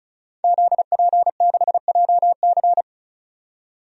Morse code, heard as a keyed tone, about 34 wpm, 700 Hz. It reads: ZP6JC
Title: DXCC-100 Call Signs - 5 Characters - 35wpm